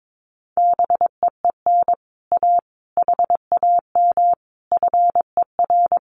Morse code, heard as a keyed tone, about 22 wpm, 700 Hz. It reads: BEEN A HAM FER
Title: Common QSO Phrases Encore 22wpm